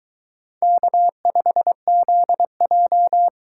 Morse code, heard as a keyed tone, about 22 wpm, 700 Hz. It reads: K5ZJ